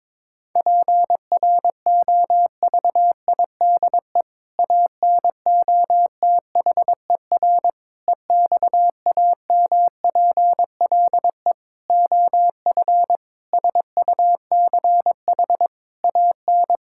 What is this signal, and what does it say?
Morse code, heard as a keyed tone, about 22 wpm, 700 Hz. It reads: PROVIDE ANOTHER EXAMPLE OF SUCH AN
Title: Sentences from Top 1000 Words - Review Part 5 - 22wpm